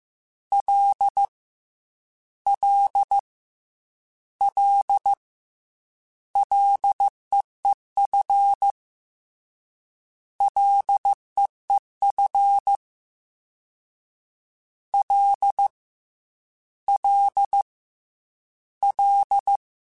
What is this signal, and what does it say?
Morse code, heard as a keyed tone, about 15 wpm, 800 Hz. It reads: L L L LEEF LEEF L L L
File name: Ls / Single L m3s / Single L Prefixe 3x2 Twice @13